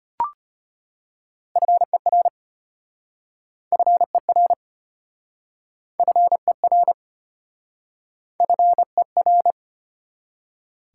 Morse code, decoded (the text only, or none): FER FER FER FER